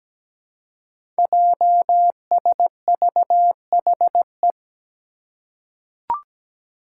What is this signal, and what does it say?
Morse code, heard as a keyed tone, about 17 wpm, 700 Hz. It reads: JSVHE
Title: Five Challenging Letter-Number Combinations 17wpm